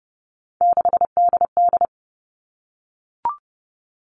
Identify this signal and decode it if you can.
Morse code, heard as a keyed tone, about 30 wpm, 700 Hz. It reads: 6DD
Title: All Cut Numbers - 3-Digit 30wpm